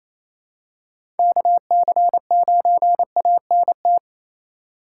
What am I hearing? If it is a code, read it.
Morse code, heard as a keyed tone, about 27 wpm, 700 Hz. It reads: KC9ANT
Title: USA Call Signs - 6 Characters - 2x Character Spacing - 28wpm